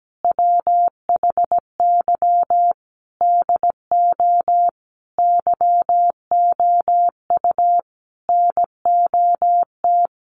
Morse code, heard as a keyed tone, about 17 wpm, 700 Hz. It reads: WHY DO YOU NOT
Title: Sentences from Top 1200 Words 17wpm